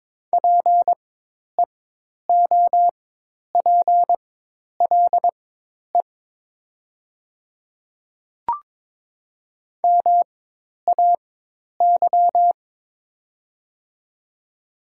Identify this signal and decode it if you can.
Morse code, heard as a keyed tone, about 22 wpm, spaced as for 10 wpm, 700 Hz. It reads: PEOPLE MAY
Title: 4x Character Spacing - Top 100 Words - 22wpm